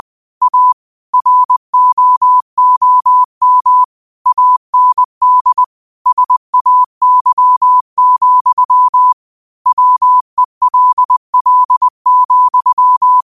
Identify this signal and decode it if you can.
Morse code, heard as a keyed tone, about 20 wpm, 1000 Hz. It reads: A ROOM AND SAY, WELL,
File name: qotdfiles / 20wpm